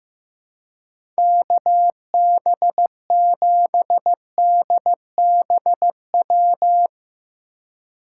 Morse code, heard as a keyed tone, about 15 wpm, 700 Hz. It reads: KB7DBW
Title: USA Call Signs - 6 Characters - 4x Character Spacing - 15wpm